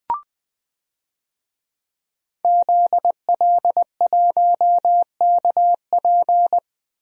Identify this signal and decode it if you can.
Morse code, heard as a keyed tone, about 20 wpm, 700 Hz. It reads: ZL1KP
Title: DXCC-100 Call Signs - 5 Characters - Repeated 3x - 20wpm